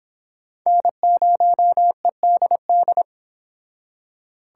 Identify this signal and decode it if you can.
Morse code, heard as a keyed tone, about 26 wpm, 700 Hz. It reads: N0EDD